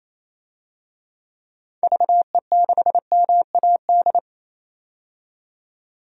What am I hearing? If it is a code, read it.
Morse code, heard as a keyed tone, about 28 wpm, 700 Hz. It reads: VE6MAD
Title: DXCC-100 Call Signs - 6 Characters - 28wpm